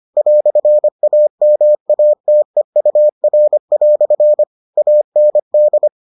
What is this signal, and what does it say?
Morse code, heard as a keyed tone, about 25 wpm, 600 Hz. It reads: "AMATEUR" AND